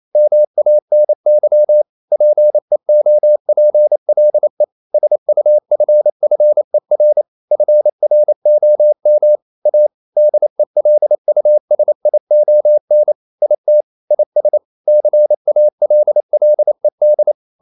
Morse code, heard as a keyed tone, about 28 wpm, 600 Hz. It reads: MANY PEOPLE SUFFER FROM A DELUSION IT IS CALLED
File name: CW / Practice / EnglishQuotations0128